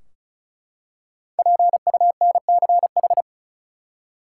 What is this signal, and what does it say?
Morse code, heard as a keyed tone, about 34 wpm, 700 Hz. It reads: PUNCH
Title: Five Letter Words - 35wpm